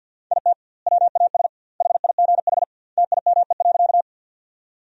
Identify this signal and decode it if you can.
Morse code, heard as a keyed tone, about 50 wpm, 700 Hz. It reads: IT WAS HIGH TIME.
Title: 2x Word-Spacing - Sentences from Top 200 Words - 50wpm